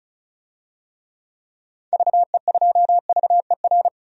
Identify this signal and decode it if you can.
Morse code, heard as a keyed tone, about 36 wpm, 700 Hz. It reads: VE2VER
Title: DXCC-100 Call Signs - 6 Characters - 35wpm